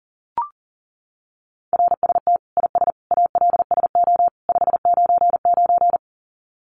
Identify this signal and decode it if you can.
Morse code, heard as a keyed tone, about 40 wpm, 700 Hz. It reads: RST IS ALSO 599
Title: Common QSO Phrases 40wpm